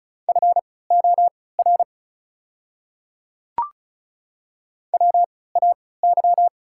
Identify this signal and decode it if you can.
Morse code, heard as a keyed tone, about 36 wpm, spaced as for 20 wpm, 690 Hz. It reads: FOR WAY
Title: 3x Character Spacing - Top 100 Words - 35wpm